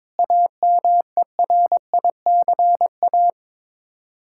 Morse code, heard as a keyed tone, about 21 wpm, 700 Hz. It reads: AMERICA
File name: practice / 22wpm / Build A Phrase - 5-Word Phrases - 22wpm